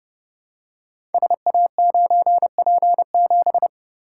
Morse code, heard as a keyed tone, about 30 wpm, 700 Hz. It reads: SA9P7
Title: Code Groups - 5 Letter:Numbers - 30wpm